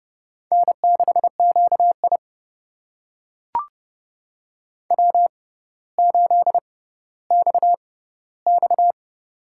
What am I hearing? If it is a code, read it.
Morse code, heard as a keyed tone, about 30 wpm, spaced as for 15 wpm, 700 Hz. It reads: N6QI W8XX
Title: USA Call Signs - 4 Characters - 6x Character Spacing - 30wpm